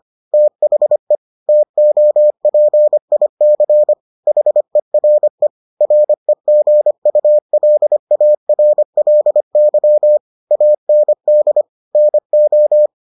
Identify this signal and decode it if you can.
Morse code, heard as a keyed tone, about 25 wpm, 600 Hz. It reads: THE TOPIC HERE REGULARLY AND NO